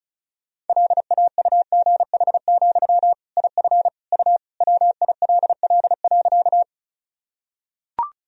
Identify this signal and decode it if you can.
Morse code, heard as a keyed tone, about 34 wpm, 700 Hz. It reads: LAUGH, IF U WILL.
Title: Sentences from Top 500 Words and Abbreviations 35wpm